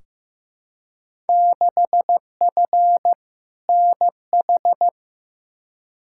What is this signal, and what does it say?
Morse code, heard as a keyed tone, about 16 wpm, 700 Hz. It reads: 6F NH
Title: Field Day Exchanges Encore - 15wpm